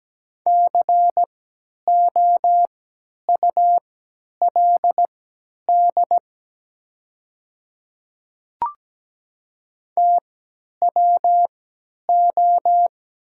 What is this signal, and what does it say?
Morse code, heard as a keyed tone, about 17 wpm, spaced as for 10 wpm, 700 Hz. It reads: COULD TWO
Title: 3x Character Spacing - Top 200 Words - 17wpm